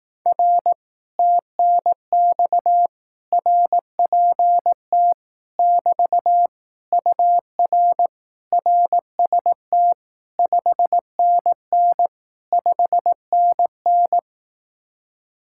Morse code, heard as a keyed tone, about 18 wpm, 700 Hz. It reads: R TNX RPT <BT> UR RST 5NN 5NN
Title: Speed-Racing Common QSO Phrases Encore - 15wpm